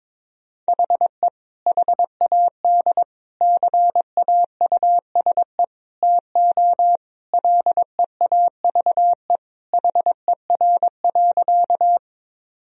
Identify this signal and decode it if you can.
Morse code, heard as a keyed tone, about 22 wpm, 700 Hz. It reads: HE HAD CAUSE TO LEAVE HER.
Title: Sentences from Top 1000 Words - Review Part 1 - 22wpm